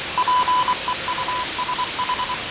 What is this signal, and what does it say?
Morse code, heard as a keyed tone, about 24 wpm, 1000 Hz, with radio noise. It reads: PEUSH
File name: fav22-6824usb-20251029-1420z-voice-test-win10